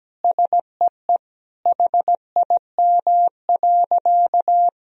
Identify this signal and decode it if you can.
Morse code, heard as a keyed tone, about 17 wpm, 700 Hz. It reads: SEE HIM.
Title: Sentences from Top 1000 Words - Review Part 3 - 17wpm